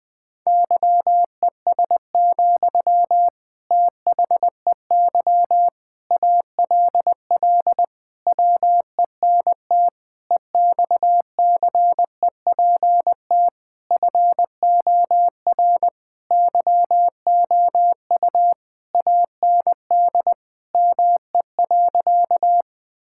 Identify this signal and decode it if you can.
Morse code, heard as a keyed tone, about 20 wpm, 700 Hz. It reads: YES, THEY ALL WENT EXCEPT FOR YOU AND ME.
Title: Sentences from Top 1600 Words 20wpm